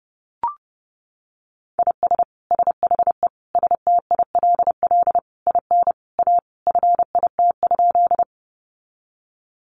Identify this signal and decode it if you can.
Morse code, heard as a keyed tone, about 30 wpm, 700 Hz. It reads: IS SHE STILL IN A FIT?
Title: Sentences from Top 800 Words 30wpm